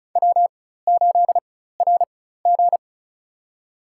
Morse code, heard as a keyed tone, about 34 wpm, spaced as for 16 wpm, 700 Hz. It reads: W8RG
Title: USA Call Signs - 4 Characters - 4x Character Spacing - 35wpm